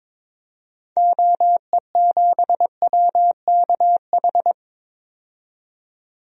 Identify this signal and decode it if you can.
Morse code, heard as a keyed tone, about 21 wpm, 700 Hz. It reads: OE7WKH
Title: DXCC-100 Call Signs - 6 Characters - 22wpm